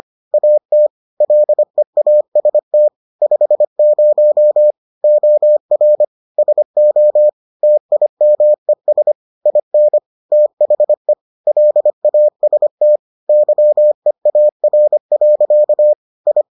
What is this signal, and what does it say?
Morse code, heard as a keyed tone, about 25 wpm, 600 Hz. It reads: AT LEAST 50 OR SO TIMES IN THE LAST YEAR. I